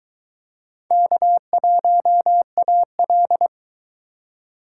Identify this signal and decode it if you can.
Morse code, heard as a keyed tone, about 23 wpm, 700 Hz. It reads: K1AL